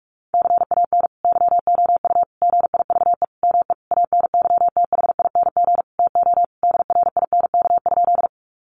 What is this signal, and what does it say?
Morse code, heard as a keyed tone, about 44 wpm, 700 Hz. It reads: CAN YOU GIVE ME ANYTHING TO DRINK?
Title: Sentences from Top 1500 Words 45wpm